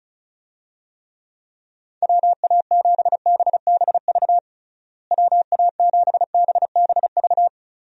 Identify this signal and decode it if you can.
Morse code, heard as a keyed tone, about 34 wpm, 700 Hz. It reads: WA7BBV WA7BBV
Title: USA Call Signs - 6 Characters - Repeated 3x - 35wpm